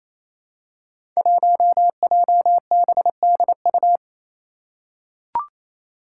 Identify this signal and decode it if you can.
Morse code, heard as a keyed tone, about 28 wpm, 700 Hz. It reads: 1JBDU
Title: Five Challenging Letter-Number Combinations 28wpm